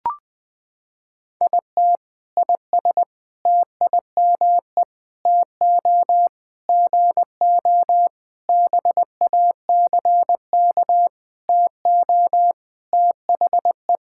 Morse code, heard as a keyed tone, about 20 wpm, 700 Hz. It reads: IT IS TIME TO GO BACK TO THE
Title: Sentences with 7-Word Phrases - 20wpm